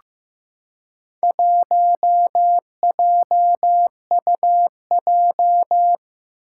Morse code, heard as a keyed tone, about 15 wpm, 700 Hz. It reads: 1JUJ
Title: Four Challenging Letter-Number Combinations 15wpm